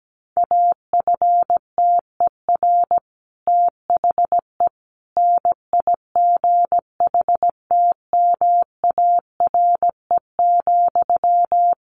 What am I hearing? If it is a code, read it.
Morse code, heard as a keyed tone, about 17 wpm, 700 Hz. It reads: AFTER THE NIGHTMARE,
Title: A Sound Of Thunder 17wpm